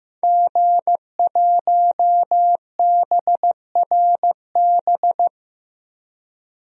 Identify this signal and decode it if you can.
Morse code, heard as a keyed tone, about 15 wpm, 700 Hz. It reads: G1BRB
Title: DXCC-100 Call Signs - 5 Characters - 15wpm